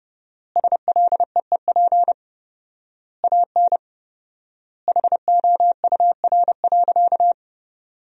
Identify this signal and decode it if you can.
Morse code, heard as a keyed tone, about 32 wpm, 700 Hz. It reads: SLEEP AN HOUR.
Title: 4x Word-Spacing - Sentences from Top 500 Words - 30wpm